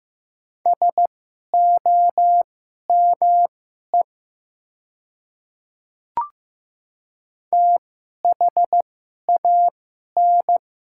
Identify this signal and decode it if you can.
Morse code, heard as a keyed tone, about 15 wpm, spaced as for 11 wpm, 700 Hz. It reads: SOME THAN
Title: 2x Character Spacing - Top 200 Words - 15wpm